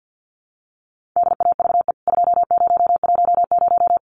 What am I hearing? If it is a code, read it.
Morse code, heard as a keyed tone, about 49 wpm, 700 Hz. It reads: DAVE 2010
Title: CWT Exchange 50wpm